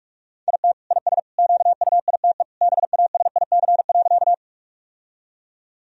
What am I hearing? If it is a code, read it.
Morse code, heard as a keyed tone, about 42 wpm, 700 Hz. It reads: IT IS QUITE BASIC.
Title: Sentences from Top 1000 Words - Review Part 3 - 45wpm